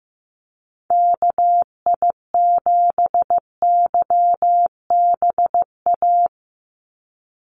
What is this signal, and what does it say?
Morse code, heard as a keyed tone, about 15 wpm, 700 Hz. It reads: KI7YBA
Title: USA Call Signs 15wpm